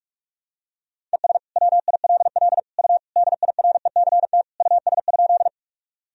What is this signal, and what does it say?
Morse code, heard as a keyed tone, about 46 wpm, 700 Hz. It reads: ES WILL U DIRECT US?
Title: Sentences from Top 400 Words and Abbreviations 45wpm